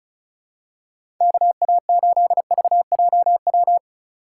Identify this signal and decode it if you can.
Morse code, heard as a keyed tone, about 36 wpm, 700 Hz. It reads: KA8VJW
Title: USA Call Signs - 6 Characters - 3x Character Spacing - 35wpm